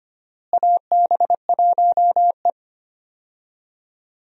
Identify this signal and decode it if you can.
Morse code, heard as a keyed tone, about 25 wpm, 700 Hz. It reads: AB1E
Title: USA Call Signs - 4 Characters - Repeated 3x - 25wpm